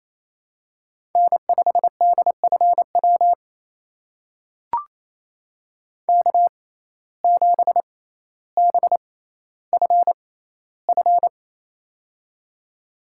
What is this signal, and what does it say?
Morse code, heard as a keyed tone, about 29 wpm, spaced as for 15 wpm, 700 Hz. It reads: N5DFW K7BFF
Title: USA Call Signs - 5 Characters - 6x Character Spacing - 28wpm